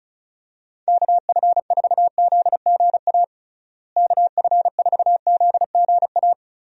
Morse code, heard as a keyed tone, about 36 wpm, 700 Hz. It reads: KF4ZGA KF4ZGA
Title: USA Call Signs - 6 Characters - Repeated 3x - 35wpm